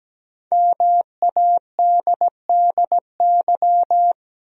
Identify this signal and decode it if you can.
Morse code, heard as a keyed tone, about 17 wpm, 700 Hz. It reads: MADDY
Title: USA Common Names - Max 5 Letters - 17wpm